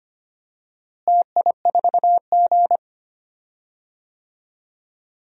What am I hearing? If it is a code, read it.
Morse code, heard as a keyed tone, about 25 wpm, 700 Hz. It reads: TI4G
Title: DXCC-100 Call Signs - 4 Characters - Repeated 3x - 25wpm